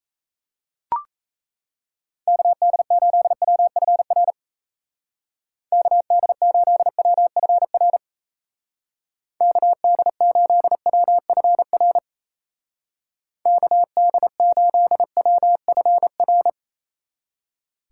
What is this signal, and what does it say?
Morse code, heard as a keyed tone, about 34 wpm, 700 Hz. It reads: KD8WFR KD8WFR KD8WFR KD8WFR
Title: Speed Racing - USA 2x3 Call Signs 28wpm